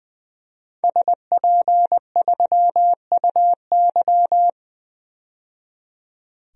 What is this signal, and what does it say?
Morse code, heard as a keyed tone, about 20 wpm, 700 Hz. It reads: SP3UY
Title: DXCC-100 Call Signs - 5 Characters - 20wpm